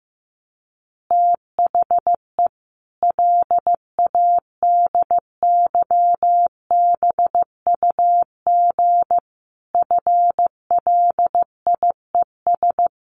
Morse code, heard as a keyed tone, about 15 wpm, 700 Hz. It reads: THE LADYBUG FLIES
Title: Additional Easy Phrases 15wpm